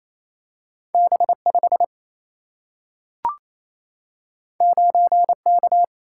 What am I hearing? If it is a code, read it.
Morse code, heard as a keyed tone, about 28 wpm, 700 Hz. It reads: B5 9K
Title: Single-Letter Single-Number 28wpm